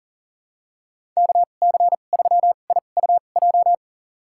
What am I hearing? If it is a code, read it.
Morse code, heard as a keyed tone, about 40 wpm, spaced as for 29 wpm, 700 Hz. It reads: KC3IUJ
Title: USA Call Signs - 6 Characters - 2x Character Spacing - 40wpm